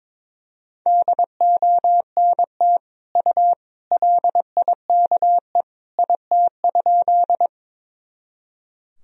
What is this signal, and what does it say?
Morse code, heard as a keyed tone, about 21 wpm, 700 Hz. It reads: DONT U LIKE IT?
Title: Sentences from Top 300 Words and Abbreviations 22wpm